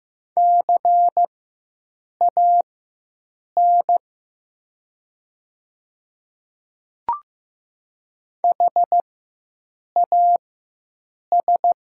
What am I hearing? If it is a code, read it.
Morse code, heard as a keyed tone, about 15 wpm, spaced as for 7 wpm, 700 Hz. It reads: CAN HAS
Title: 4x Character Spacing - Top 100 Words - 15wpm